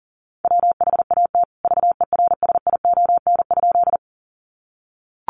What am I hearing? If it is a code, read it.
Morse code, heard as a keyed tone, about 40 wpm, 700 Hz. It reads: WHAT VERSION?
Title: Sentences from Top 1400 Words 40wpm